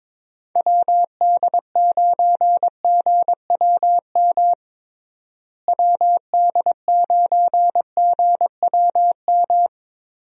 Morse code, heard as a keyed tone, about 21 wpm, 700 Hz. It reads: WD9GWM WD9GWM
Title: USA Call Signs - 6 Characters - Repeated 2x - 22wpm